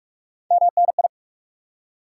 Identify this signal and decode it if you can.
Morse code, heard as a keyed tone, about 44 wpm, 700 Hz. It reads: MNI
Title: Top 90 QSO Elements 45wpm